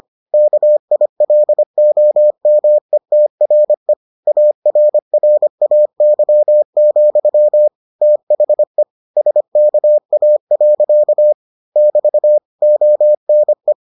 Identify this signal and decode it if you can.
Morse code, heard as a keyed tone, about 25 wpm, 600 Hz. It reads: KILOMETRE ARRAY, THE SKA. <BT> ONE